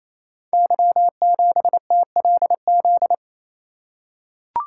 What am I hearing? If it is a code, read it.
Morse code, heard as a keyed tone, about 27 wpm, 700 Hz. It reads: Y7TLZ
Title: Code Groups - 5 Letter:Numbers - 28wpm